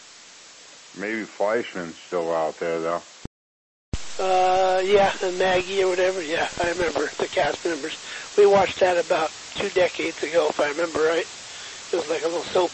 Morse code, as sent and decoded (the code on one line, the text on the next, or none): none